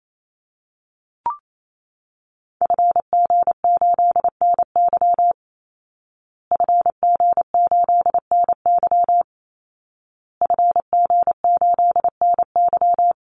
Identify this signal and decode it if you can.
Morse code, heard as a keyed tone, about 28 wpm, 700 Hz. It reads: FG8NY FG8NY FG8NY
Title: Call Signs Repeated 2x - 28wpm